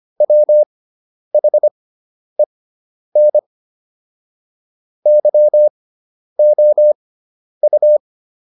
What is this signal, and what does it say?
Morse code, heard as a keyed tone, about 25 wpm, spaced as for 10 wpm, 600 Hz. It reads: WHEN YOU